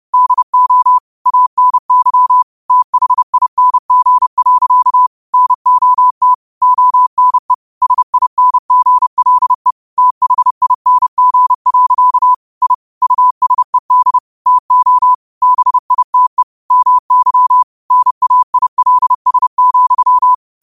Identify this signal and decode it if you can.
Morse code, heard as a keyed tone, about 30 wpm, 1000 Hz. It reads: DO ANYTHING. NOT ONE SINGLE THING. I USED TO BITE MY NAILS,
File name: qotdfiles / 30wpm